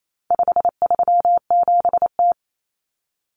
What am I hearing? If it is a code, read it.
Morse code, heard as a keyed tone, about 28 wpm, 700 Hz. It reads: 537T
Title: Common Cut Numbers - 4-Digit 28wpm